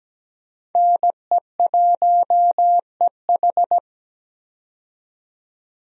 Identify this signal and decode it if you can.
Morse code, heard as a keyed tone, about 17 wpm, 700 Hz. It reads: NE1EH
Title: USA Call Signs - 5 Characters - 17wpm